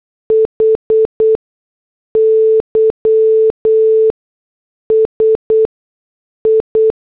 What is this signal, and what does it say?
Morse code, heard as a keyed tone, about 8 wpm, spaced as for 6 wpm, 440 Hz. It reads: HYSI
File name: morse / headlines